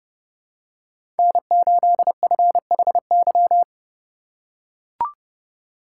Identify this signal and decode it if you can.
Morse code, heard as a keyed tone, about 30 wpm, 700 Hz. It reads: N8FHY
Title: USA Call Signs - 5 Characters - 4x Character Spacing - 30wpm